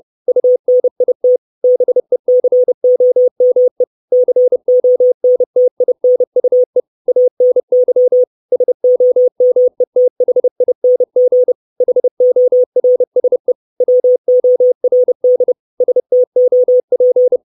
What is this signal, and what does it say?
Morse code, heard as a keyed tone, about 30 wpm, 500 Hz. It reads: UNIT BECOME CONTINUE ANY SOMETHING HORSE WORD STOP
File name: 632 Words 30WPM